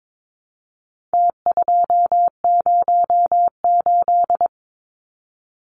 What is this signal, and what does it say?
Morse code, heard as a keyed tone, about 22 wpm, 700 Hz. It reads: T208
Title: Numbers - Contest Style - 4-Digit 22wpm